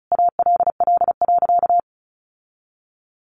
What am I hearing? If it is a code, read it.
Morse code, heard as a keyed tone, about 34 wpm, 700 Hz. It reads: ALL.
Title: Sentences from Top 1300 Words 35wpm